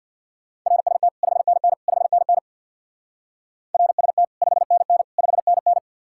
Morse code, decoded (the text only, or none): RST 5NN 5NN RST 5NN 5NN